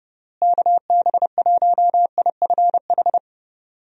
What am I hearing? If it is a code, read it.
Morse code, heard as a keyed tone, about 32 wpm, 700 Hz. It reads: KB1IFH
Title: USA Call Signs - 6 Characters - Repeated 3x - 30wpm